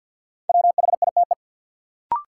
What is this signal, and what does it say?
Morse code, heard as a keyed tone, about 53 wpm, 700 Hz. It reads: WHITE
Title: Sets of 2 Words - New Word Drill - 50wpm